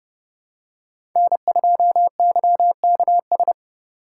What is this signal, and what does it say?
Morse code, heard as a keyed tone, about 30 wpm, 700 Hz. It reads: N2YKS
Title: USA Call Signs - 4x Character Spacing - 30wpm